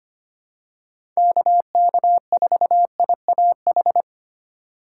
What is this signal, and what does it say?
Morse code, heard as a keyed tone, about 25 wpm, 700 Hz. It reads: KK4IAH
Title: USA Call Signs - 6 Characters - 2x Character Spacing - 25wpm